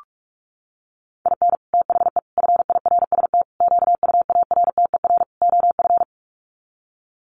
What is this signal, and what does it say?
Morse code, heard as a keyed tone, about 44 wpm, 700 Hz. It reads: IN THE FIRST QUARTER OF
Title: Sets of 5 Words Encore2 45wpm